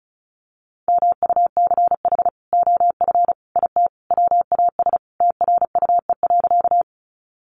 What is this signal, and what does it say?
Morse code, heard as a keyed tone, about 34 wpm, 700 Hz. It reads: MUCH OF IT WAS TRUE.
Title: Sentences from Top 400 Words 35wpm